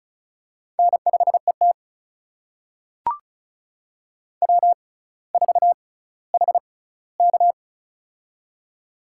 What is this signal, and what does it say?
Morse code, heard as a keyed tone, about 36 wpm, spaced as for 18 wpm, 700 Hz. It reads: N5ET W4HK
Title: USA Call Signs - 4 Characters - 6x Character Spacing - 35wpm